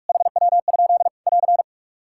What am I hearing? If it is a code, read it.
Morse code, heard as a keyed tone, about 46 wpm, 700 Hz. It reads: HW? <AR>
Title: Practice QSO - Exchange 3 - More Info - 45wpm